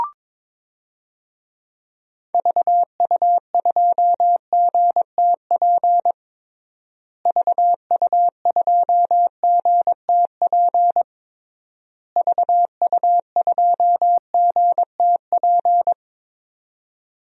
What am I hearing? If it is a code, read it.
Morse code, heard as a keyed tone, about 21 wpm, 700 Hz. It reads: VU2GTP VU2GTP VU2GTP
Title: DXCC-100 Call Signs - 6 Characters - Repeated 3x - 22wpm